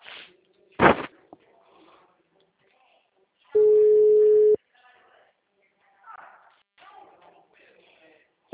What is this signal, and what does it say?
Ringback tone (Europe, 425 Hz), with speech.